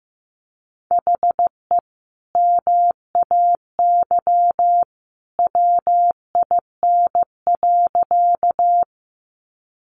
Morse code, heard as a keyed tone, about 15 wpm, 700 Hz. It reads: HE MAY WIN.
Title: Sentences from Top 1500 Words 15wpm